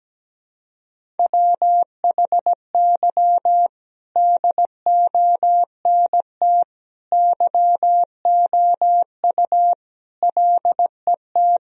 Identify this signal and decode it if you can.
Morse code, heard as a keyed tone, about 17 wpm, 700 Hz. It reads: WHY DONT YOU LET
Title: Sentences from Top 1000 Words - Review Part 1 - 17wpm